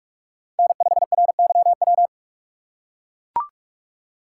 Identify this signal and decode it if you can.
Morse code, heard as a keyed tone, about 44 wpm, 690 Hz. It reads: N5RYW
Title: USA Call Signs - 5 Characters - 6x Character Spacing - 45wpm